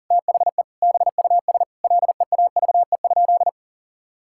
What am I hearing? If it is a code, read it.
Morse code, heard as a keyed tone, about 37 wpm, 700 Hz. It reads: THE BUS LEAVE?
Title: Sentences from Top 1000 Words - Review Part 6 - 40wpm